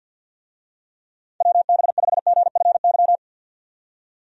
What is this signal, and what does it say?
Morse code, heard as a keyed tone, about 50 wpm, 700 Hz. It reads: WB5GFY
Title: USA Call Signs - 6 Characters - 50wpm